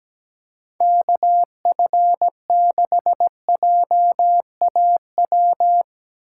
Morse code, heard as a keyed tone, about 17 wpm, 700 Hz. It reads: KF6JAW
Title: Speed Racing - USA 2x3 Call Signs 17wpm